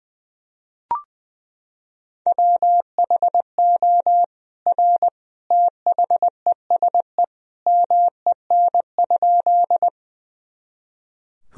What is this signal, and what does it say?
Morse code, heard as a keyed tone, about 20 wpm, 700 Hz. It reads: WHO R THESE MEN?
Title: Sentences from Top 200 Words and Abbreviations 20wpm